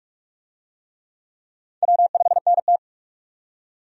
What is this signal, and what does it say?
Morse code, heard as a keyed tone, about 47 wpm, 700 Hz. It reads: W5NT
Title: USA Call Signs - 4 Characters - 45wpm